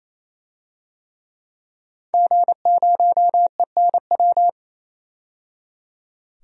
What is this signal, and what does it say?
Morse code, heard as a keyed tone, about 28 wpm, 700 Hz. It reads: G0ENW
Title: DXCC-100 Call Signs - 5 Characters - 28wpm